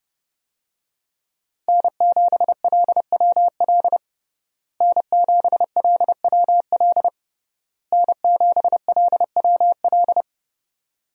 Morse code, heard as a keyed tone, about 30 wpm, 700 Hz. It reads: N7LWL N7LWL N7LWL
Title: USA Call Signs - 5 Characters - Repeated 3x - 30wpm